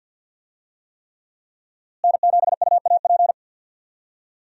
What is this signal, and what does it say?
Morse code, heard as a keyed tone, about 51 wpm, 700 Hz. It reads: N7UAP
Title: USA Call Signs - 5 Characters - Repeated 3x - 50wpm